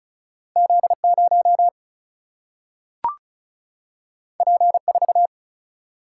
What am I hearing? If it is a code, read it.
Morse code, heard as a keyed tone, about 36 wpm, 700 Hz. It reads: Z0 P4
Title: Single-Letter Single-Number 35wpm